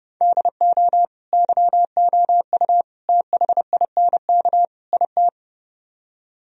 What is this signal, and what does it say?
Morse code, heard as a keyed tone, about 31 wpm, 700 Hz. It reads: DO YOU THINK IT